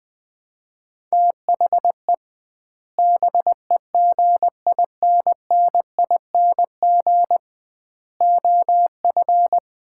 Morse code, heard as a keyed tone, about 20 wpm, 700 Hz. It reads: THE BEGINNING OF
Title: 2x Word-Spacing - Sets of 3 Words - 20wpm